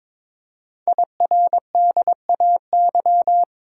Morse code, heard as a keyed tone, about 22 wpm, 700 Hz. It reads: IRDAY
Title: Code Groups - 5 Letters - 22wpm